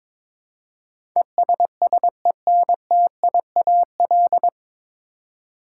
Morse code, heard as a keyed tone, about 21 wpm, 700 Hz. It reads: ESSENTIAL
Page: Morse Code Ninja practice 22wpm